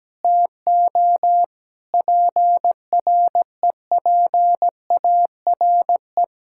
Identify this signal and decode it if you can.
Morse code, heard as a keyed tone, about 17 wpm, 700 Hz. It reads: TO PREPARE
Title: Sentences from Top 1000 Words - Review Part 6 - 17wpm